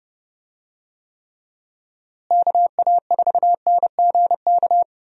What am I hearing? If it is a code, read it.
Morse code, heard as a keyed tone, about 29 wpm, 700 Hz. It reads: KA4NGK
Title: USA Call Signs - 6 Characters - Repeated 3x - 30wpm